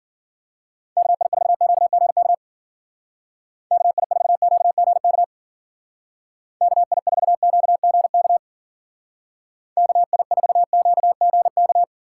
Morse code, heard as a keyed tone, about 52 wpm, 700 Hz. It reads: KI4QGK KI4QGK KI4QGK KI4QGK